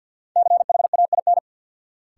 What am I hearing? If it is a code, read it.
Morse code, heard as a keyed tone, about 53 wpm, 700 Hz. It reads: CHAIN
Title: Sentences from Top 1500 Words - New Word Drill - 50wpm